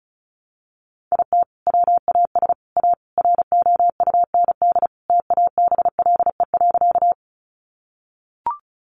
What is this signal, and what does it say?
Morse code, heard as a keyed tone, about 34 wpm, 700 Hz. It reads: IT WAS A ROUND TABLE.
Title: Sentences from Top 400 Words 35wpm